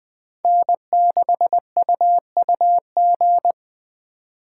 Morse code, heard as a keyed tone, about 21 wpm, 700 Hz. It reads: N6UUG